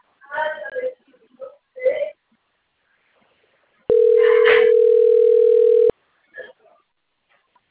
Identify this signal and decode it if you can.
Ringback tone (North America), 440 + 480 Hz, with speech.
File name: out- - - - - .63695